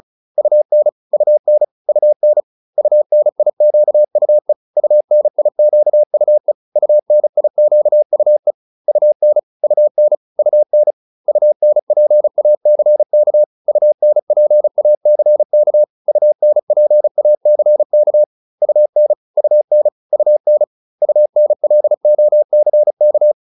Morse code, heard as a keyed tone, about 34 wpm, 600 Hz. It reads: UN UN UN UNIQUE UNIQUE UNIQUE UN UN UN UNPACK UNPACK UNPACK UN UN UN UNLOCK